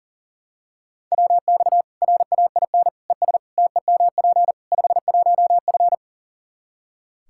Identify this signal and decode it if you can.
Morse code, heard as a keyed tone, about 37 wpm, 700 Hz. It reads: WX RAIN ES TEMP 51F
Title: Practice QSO - Exchange 2 - Info - 40wpm